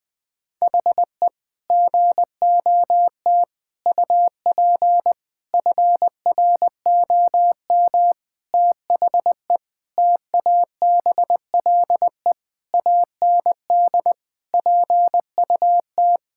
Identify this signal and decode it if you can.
Morse code, heard as a keyed tone, about 20 wpm, 700 Hz. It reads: HE GOT UP FROM THE TABLE AND PUT
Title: Sentences from Top 1600 Words 20wpm